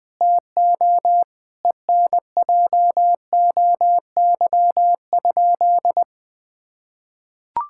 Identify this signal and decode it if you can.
Morse code, heard as a keyed tone, about 21 wpm, 700 Hz. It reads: TO ENJOY?
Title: Sentences from Top 1000 Words - Review Part 5 - 20wpm